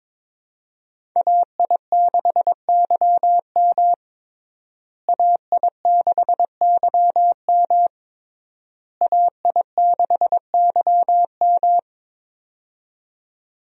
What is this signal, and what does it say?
Morse code, heard as a keyed tone, about 22 wpm, 700 Hz. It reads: AI6YM AI6YM AI6YM
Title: USA Call Signs - 5 Characters - Repeated 3x - 22wpm